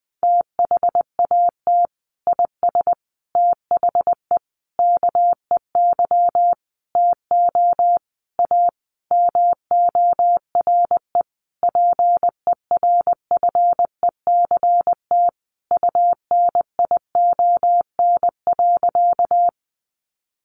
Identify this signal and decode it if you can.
Morse code, heard as a keyed tone, about 20 wpm, 700 Hz. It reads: THAT IS THE KEY TO A MORE PERFECT UNION.
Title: Sentences from Top 1400 Words 20wpm